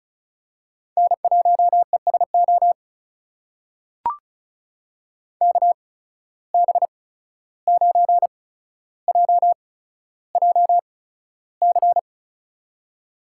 Morse code, heard as a keyed tone, about 36 wpm, spaced as for 14 wpm, 700 Hz. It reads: N1ESO KB9JJC
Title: USA Call Signs - 8x Character Spacing - 35wpm